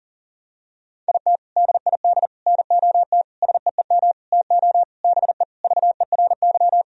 Morse code, heard as a keyed tone, about 37 wpm, 700 Hz. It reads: IT DID NOT SEEM TO BE VERY
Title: Sentences from Top 1000 Words - Review Part 1 - 40wpm